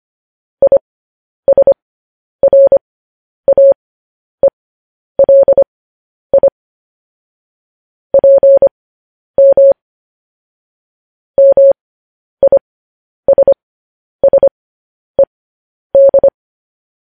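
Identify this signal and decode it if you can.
Morse code, heard as a keyed tone, about 25 wpm, spaced as for 10 wpm, 550 Hz. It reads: ISRAELI PM MISSED
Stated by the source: Morse news headlines sent at 10 words per minute.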